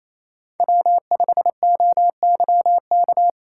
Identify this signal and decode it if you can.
Morse code, heard as a keyed tone, about 27 wpm, 700 Hz. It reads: W5OYK
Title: USA Call Signs - 5 Characters - 4x Character Spacing - 28wpm